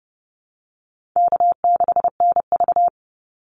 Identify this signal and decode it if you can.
Morse code, heard as a keyed tone, about 30 wpm, 700 Hz. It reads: K6NV
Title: Easy Call Signs - 30wpm